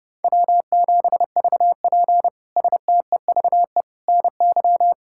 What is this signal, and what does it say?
Morse code, heard as a keyed tone, about 30 wpm, 700 Hz. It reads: W7VP STEVE NY
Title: K1USN SST QSO Practice - 30wpm